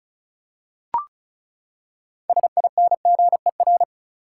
Morse code, heard as a keyed tone, about 36 wpm, 700 Hz. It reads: SINGER
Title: Six Letter Words - 35wpm